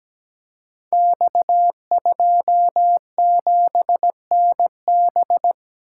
Morse code, heard as a keyed tone, about 17 wpm, 700 Hz. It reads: X27NB
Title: Code Groups - 5 Letter:Numbers - 17wpm